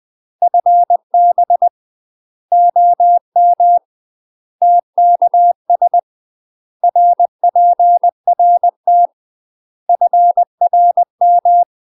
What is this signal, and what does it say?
Morse code, heard as a keyed tone, about 20 wpm, 700 Hz. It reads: FB OM TKS RPRT FRM
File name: UK MASTER QSOs TRAINING / 20WPM / UK MASTER QSOs 0058